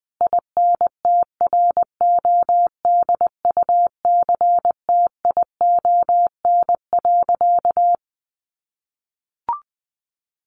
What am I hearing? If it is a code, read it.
Morse code, heard as a keyed tone, about 20 wpm, 700 Hz. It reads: INTRODUCTION.
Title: Sentences from Top 1000 Words 20wpm